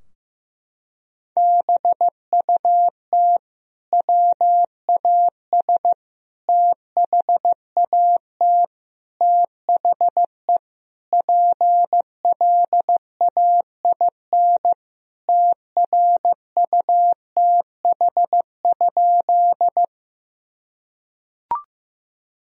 Morse code, heard as a keyed tone, about 16 wpm, 700 Hz. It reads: BUT WAS THAT THE PLAIN TRUTH?
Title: Sentences from Top 1000 Words - Review Part 2 - 15wpm